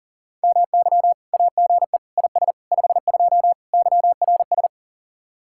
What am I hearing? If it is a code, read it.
Morse code, heard as a keyed tone, about 50 wpm, 700 Hz. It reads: MY AGE IS 52 YRS